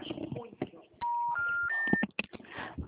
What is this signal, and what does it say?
Special information tone, with speech.
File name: out- -2023- - - .97749